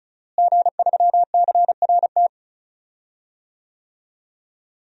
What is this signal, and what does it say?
Morse code, heard as a keyed tone, about 34 wpm, 700 Hz. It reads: G3CRT
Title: DXCC-100 Call Signs - 5 Characters - Repeated 2x - 35wpm